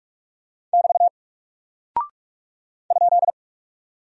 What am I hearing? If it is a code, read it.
Morse code, heard as a keyed tone, about 44 wpm, 700 Hz. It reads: <BT> ?
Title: Single Character (Letter:Number:Punctuation:Prosign) Alt 45wpm